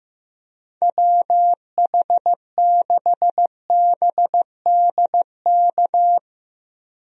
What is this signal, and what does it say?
Morse code, heard as a keyed tone, about 16 wpm, 700 Hz. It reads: WH6BDK
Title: USA Call Signs - 6 Characters - Repeated 2x - 15wpm